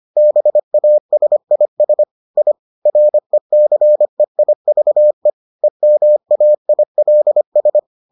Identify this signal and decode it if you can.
Morse code, heard as a keyed tone, about 25 wpm, 600 Hz. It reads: BASIS I RECEIVE EMAILS